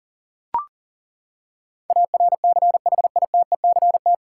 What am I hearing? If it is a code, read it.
Morse code, heard as a keyed tone, about 40 wpm, 700 Hz. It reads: ARCHITECT
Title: Practice QSO - Exchange 3 - More Info - 40wpm